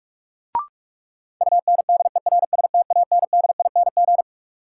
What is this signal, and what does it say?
Morse code, heard as a keyed tone, about 45 wpm, 700 Hz. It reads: UNDERSTANDING
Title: Sentences from Top 1400 Words - New Word Drill Alt - 45wpm